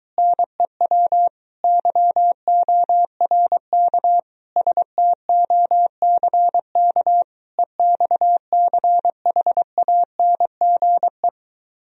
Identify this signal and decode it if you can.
Morse code, heard as a keyed tone, about 24 wpm, 700 Hz. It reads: NEW YORK STOCK EXCHANGE